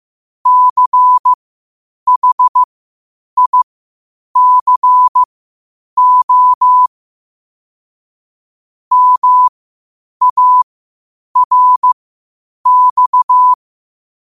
Morse code, heard as a keyed tone, about 15 wpm, spaced as for 8 wpm, 1000 Hz. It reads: CHICO MARX
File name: qotdfiles / 8wpm